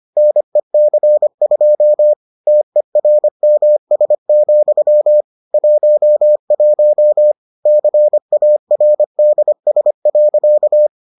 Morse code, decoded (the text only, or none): NEC2 TERMS, 11 CARDS.